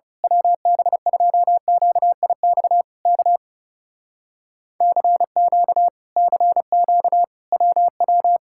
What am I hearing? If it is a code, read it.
Morse code, heard as a keyed tone, about 28 wpm, 700 Hz. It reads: WB2QIX K CQ CQ WW
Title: Speed-Racing Common QSO Phrases - 30wpm